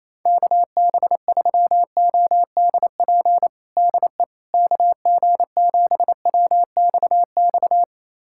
Morse code, heard as a keyed tone, about 28 wpm, 700 Hz. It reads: KB3ODP DE KG7WXX
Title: USA Callsign DE Callsign 28wpm